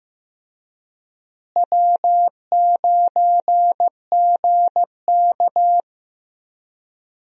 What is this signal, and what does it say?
Morse code, heard as a keyed tone, about 15 wpm, 700 Hz. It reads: W9GK
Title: USA Call Signs - 4 Characters - Repeated 3x - 15wpm